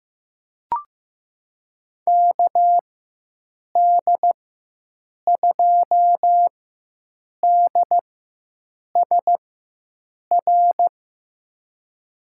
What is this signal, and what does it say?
Morse code, heard as a keyed tone, about 15 wpm, spaced as for 7 wpm, 700 Hz. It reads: KD2DSR
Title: USA Call Signs - 6 Characters - 4x Character Spacing - 15wpm